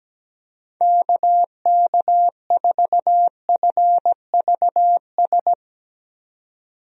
Morse code, heard as a keyed tone, about 17 wpm, 700 Hz. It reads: KK4FVS
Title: USA Call Signs - 6 Characters - 17wpm